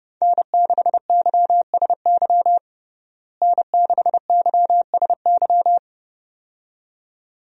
Morse code, heard as a keyed tone, about 30 wpm, 700 Hz. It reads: N6YSY N6YSY
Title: USA Call Signs - 5 Characters - Repeated 2x - 30wpm